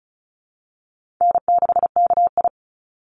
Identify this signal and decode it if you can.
Morse code, heard as a keyed tone, about 34 wpm, 700 Hz. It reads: N6KI
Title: Call Signs Repeated 3x - 35wpm